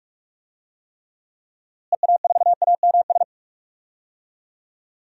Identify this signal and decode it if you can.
Morse code, heard as a keyed tone, about 42 wpm, 700 Hz. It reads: EA4AMS
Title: DXCC-100 Call Signs - 6 Characters - Repeated 3x - 45wpm